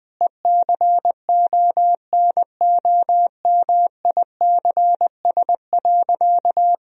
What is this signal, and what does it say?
Morse code, heard as a keyed tone, about 21 wpm, 700 Hz. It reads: ECONOMICS.
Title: Sentences from Top 1000 Words - Review Part 3 - 20wpm